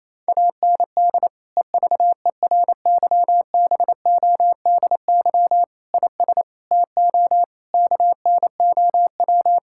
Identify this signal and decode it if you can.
Morse code, heard as a keyed tone, about 28 wpm, 700 Hz. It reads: AND EVERYBODY IS TO KNOW
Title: Sentences from Top 1600 Words 28wpm